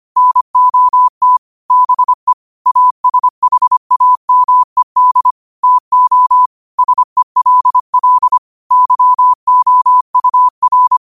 Morse code, heard as a keyed tone, about 25 wpm, 1000 Hz. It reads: NOT BE ASHAMED TO SELL YOUR